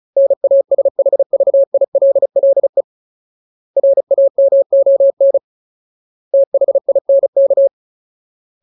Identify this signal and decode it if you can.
Morse code, heard as a keyed tone, about 34 wpm, 550 Hz. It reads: NASHVILLE RAMON THINK
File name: Ultimate Word List / 35WPM Ultimate